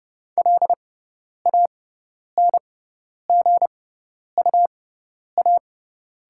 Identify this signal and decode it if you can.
Morse code, heard as a keyed tone, about 30 wpm, spaced as for 10 wpm, 690 Hz. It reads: LANGUA
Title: 6x Character Spacing - Top 500 Words - 30wpm